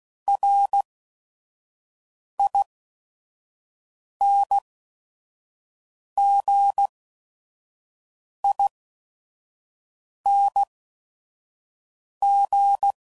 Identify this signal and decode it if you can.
Morse code, heard as a keyed tone, about 16 wpm, spaced as for 5 wpm, 800 Hz. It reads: RINGING